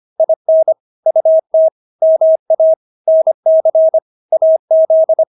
Morse code, heard as a keyed tone, about 25 wpm, 640 Hz. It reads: IN UT MA NC AZ
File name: State-Abbreviations-25